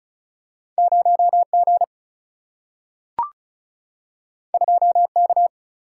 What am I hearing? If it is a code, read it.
Morse code, heard as a keyed tone, about 36 wpm, 700 Hz. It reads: 0 G 2 K
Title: Single-Letter Single-Number 35wpm